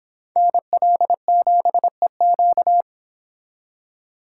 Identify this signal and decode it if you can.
Morse code, heard as a keyed tone, about 26 wpm, 700 Hz. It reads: NL7EQ